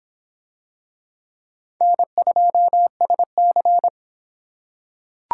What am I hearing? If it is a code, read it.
Morse code, heard as a keyed tone, about 24 wpm, 700 Hz. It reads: N2SC